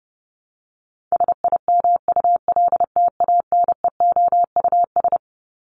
Morse code, heard as a keyed tone, about 30 wpm, 700 Hz. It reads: SIMULTANEOUS
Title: LY Suffix - Top 100 Words with Base Words 30wpm